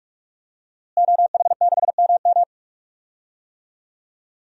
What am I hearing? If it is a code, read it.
Morse code, heard as a keyed tone, about 46 wpm, 700 Hz. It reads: OH6MM
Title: DXCC-100 Call Signs - 5 Characters - Repeated 3x - 45wpm